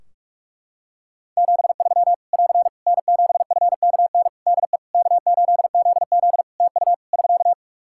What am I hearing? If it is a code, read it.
Morse code, heard as a keyed tone, about 46 wpm, 700 Hz. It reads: 73 <AR> N7FKN DE K8ZZ TU <SK>
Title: Practice QSO - Exchange 4 - Ending - 45wpm